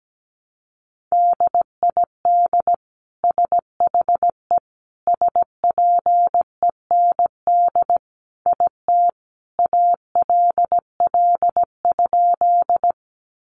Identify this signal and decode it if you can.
Morse code, heard as a keyed tone, about 17 wpm, 700 Hz. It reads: DID SHE SPEND IT ALL?
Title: Sentences from Top 800 Words 17wpm